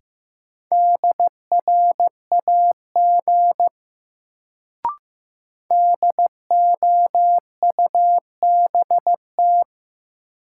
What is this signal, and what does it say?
Morse code, heard as a keyed tone, about 15 wpm, 700 Hz. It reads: DRAG DOUBT
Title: Sentences from Top 1500 Words - New Word Drill Alt - 15wpm